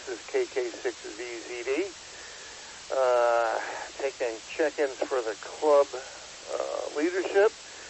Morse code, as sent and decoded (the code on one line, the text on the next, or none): none